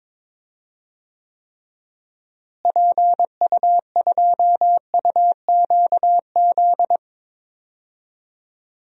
Morse code, heard as a keyed tone, about 22 wpm, 700 Hz. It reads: PU2UQZ
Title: DXCC-100 Call Signs - 6 Characters - Repeated 3x - 22wpm